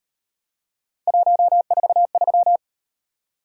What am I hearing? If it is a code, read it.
Morse code, heard as a keyed tone, about 36 wpm, 700 Hz. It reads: 143